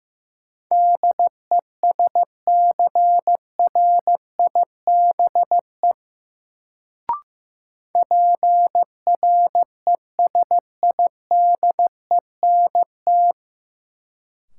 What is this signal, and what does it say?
Morse code, heard as a keyed tone, about 15 wpm, 700 Hz. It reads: DESCRIBE PRESIDENT
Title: Sentences from Top 1000 Words - New Word Drill - 15wpm